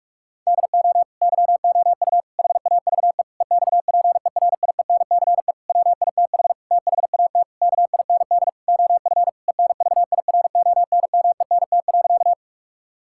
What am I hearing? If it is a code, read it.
Morse code, heard as a keyed tone, about 46 wpm, 700 Hz. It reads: DO YOU HAVE EXPERIENCE WITH THAT KIND OF ENVIRONMENT.
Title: Sentences from Top 1000 Words - Review Part 5 - 45wpm